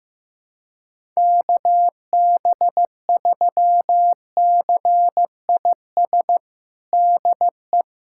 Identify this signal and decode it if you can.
Morse code, heard as a keyed tone, about 15 wpm, 700 Hz. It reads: KB3CIS DE
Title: USA Callsign DE Callsign - 6 Characters - 15wpm